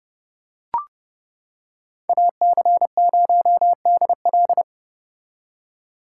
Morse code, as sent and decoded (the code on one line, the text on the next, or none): .- -.-. ----- -.. .-..
AC0DL